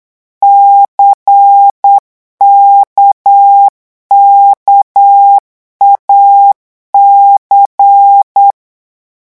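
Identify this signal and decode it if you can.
Morse code, heard as a keyed tone, about 8 wpm, 800 Hz. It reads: CKKAC
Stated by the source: Morse code revision lesson at 8 words per minute, from a French amateur radio course.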